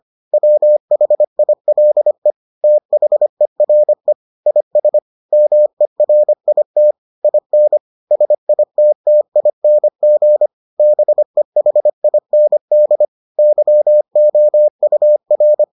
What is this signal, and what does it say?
Morse code, heard as a keyed tone, about 25 wpm, 600 Hz. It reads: WHILE THERE IS MERIT IN SITTING BEHIND YOUR